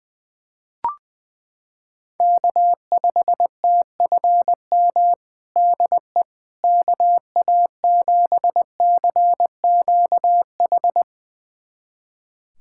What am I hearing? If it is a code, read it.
Morse code, heard as a keyed tone, about 20 wpm, 700 Hz. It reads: K5TFM DE KA7CQH
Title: USA Callsign DE Callsign 20wpm